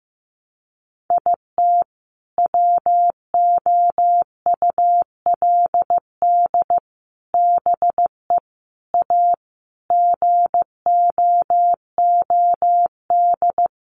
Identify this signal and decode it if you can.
Morse code, heard as a keyed tone, about 15 wpm, 700 Hz. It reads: IT WOULD BE A GOOD
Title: Sets of 5 Words Encore2 15wpm